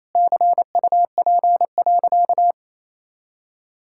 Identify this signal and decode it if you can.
Morse code, heard as a keyed tone, about 28 wpm, 700 Hz. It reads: CUP.
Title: Sentences from Top 1600 Words 28wpm